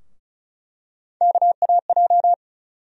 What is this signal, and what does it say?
Morse code, heard as a keyed tone, about 36 wpm, 700 Hz. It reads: KAJ
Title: USA Call Sign Postfixes - 3 Letters 35wpm